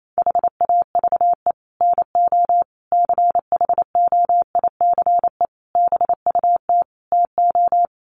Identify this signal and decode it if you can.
Morse code, heard as a keyed tone, about 28 wpm, 700 Hz. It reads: HAVE NO CHOICE BUT TO
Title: Sets of 5 Words Encore 28wpm